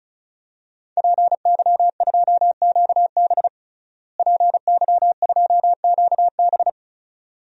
Morse code, heard as a keyed tone, about 34 wpm, 700 Hz. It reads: PY2QB PY2QB
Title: DXCC-100 Call Signs - 5 Characters - Repeated 2x - 35wpm